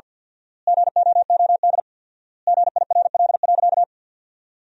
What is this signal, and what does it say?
Morse code, heard as a keyed tone, about 47 wpm, 700 Hz. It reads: GOOD GIRL.
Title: 4x Word-Spacing - Sentences from Top 400 Words - 50wpm